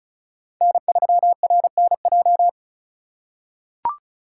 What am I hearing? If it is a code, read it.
Morse code, heard as a keyed tone, about 34 wpm, 700 Hz. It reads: N3RNJ
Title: USA Call Signs - 5 Characters - 4x Character Spacing - 35wpm